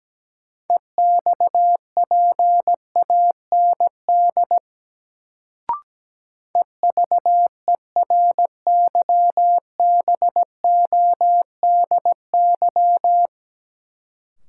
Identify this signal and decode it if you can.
Morse code, heard as a keyed tone, about 17 wpm, 700 Hz. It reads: EXPAND EVERYBODY
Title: Sentences from Top 1400 Words - New Word Drill - 17wpm